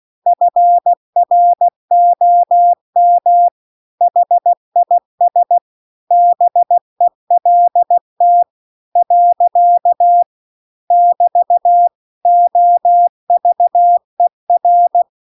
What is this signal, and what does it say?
Morse code, heard as a keyed tone, about 16 wpm, 700 Hz. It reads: FROM HIS BELT . <BT> OVER
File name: CW-The-Big-Twoheaded-River-2-16WPM